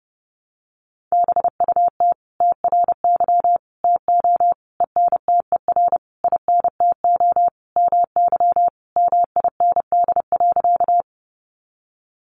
Morse code, heard as a keyed tone, about 30 wpm, 700 Hz. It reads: BUT TRY TO ENTER INTO MY MIND.
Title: Sentences from Top 1300 Words 30wpm